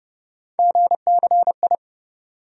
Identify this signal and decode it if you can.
Morse code, heard as a keyed tone, about 29 wpm, 700 Hz. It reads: GCI
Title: USA Call Sign Postfixes - 3 Letters 30wpm